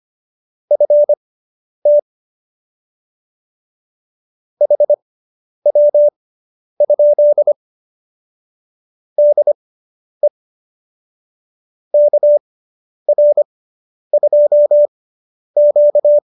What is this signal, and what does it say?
Morse code, heard as a keyed tone, about 25 wpm, spaced as for 10 wpm, 590 Hz. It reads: FT HW? DE KR2Q